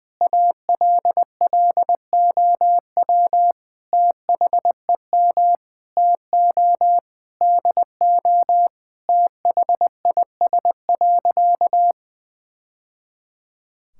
Morse code, heard as a keyed tone, about 20 wpm, 700 Hz. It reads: ALLOW THEM TO DO THIS.
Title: Sentences from Top 1000 Words - Review Part 5 - 20wpm